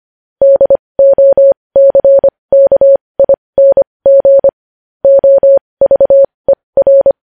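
Morse code, heard as a keyed tone, about 25 wpm, spaced as for 20 wpm, 550 Hz. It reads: DOCKING OVER
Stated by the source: Morse code news headlines sent at 20.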